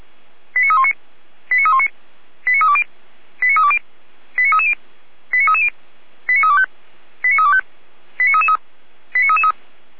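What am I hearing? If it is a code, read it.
Five-tone selective call (ZVEI), heard as quick runs of five tones, tones of about 70 ms. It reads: 89319 89319 89320 89320 89330 89330 89325 89325 89333
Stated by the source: ZVEI five-tone call received by SDR.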